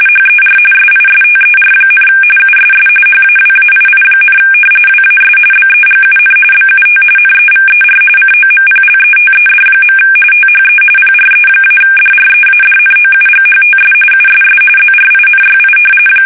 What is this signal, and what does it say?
RTTY, about 45.45 baud. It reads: HER SISTER, USS HAWAII, WAS BEGUN BUT NEVER COMPLETED. THE PRE-DREADNOUGHTS USS ZRINYI
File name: fsk850 / battleships